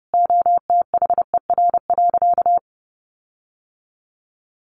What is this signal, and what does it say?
Morse code, heard as a keyed tone, about 30 wpm, 700 Hz. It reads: OTHER.
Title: Sentences from Top 1300 Words 30wpm